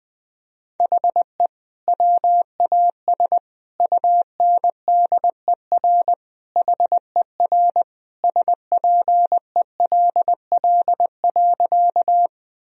Morse code, heard as a keyed tone, about 21 wpm, 700 Hz. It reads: HE WAS UNDER HER SPELL.
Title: Sentences from Top 1000 Words - Review Part 1 - 20wpm